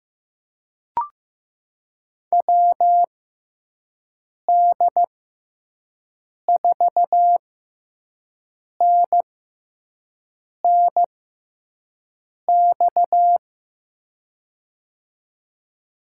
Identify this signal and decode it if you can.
Morse code, heard as a keyed tone, about 15 wpm, spaced as for 5 wpm, 700 Hz. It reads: WD4NNX
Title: USA Call Signs - 6x Character Spacing - 15wpm